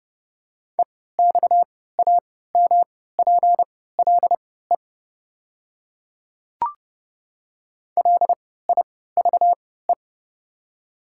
Morse code, heard as a keyed tone, about 30 wpm, spaced as for 17 wpm, 690 Hz. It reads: EXAMPLE LIVE
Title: 3x Character Spacing - Top 300 Words - 30wpm